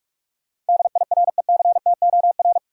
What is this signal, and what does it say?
Morse code, heard as a keyed tone, about 46 wpm, 700 Hz. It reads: DIRECTOR
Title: Sentences from Top 1300 Words - New Word Drill Alt - 45wpm